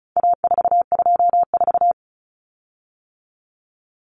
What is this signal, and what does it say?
Morse code, heard as a keyed tone, about 34 wpm, 700 Hz. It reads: A424
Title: Common Cut Numbers - 4-Digit 35wpm